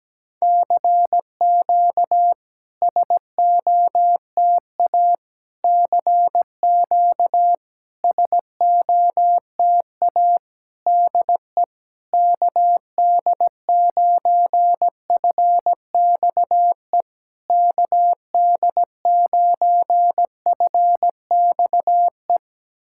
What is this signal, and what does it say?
Morse code, heard as a keyed tone, about 20 wpm, 700 Hz. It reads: CQ SOTA CQ SOTA DE KD9FXE KD9FXE
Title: SOTA QSO Practice 17wpm